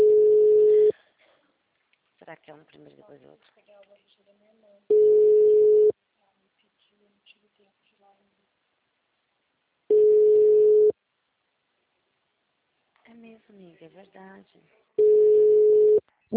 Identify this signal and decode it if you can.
Ringback tone (Europe, 425 Hz), with speech.
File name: out- -2009- - - .4574